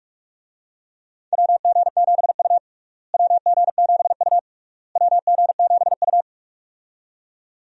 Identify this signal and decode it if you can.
Morse code, heard as a keyed tone, about 46 wpm, 700 Hz. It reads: WG7U WG7U WG7U
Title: USA Call Signs - 4 Characters - Repeated 3x - 45wpm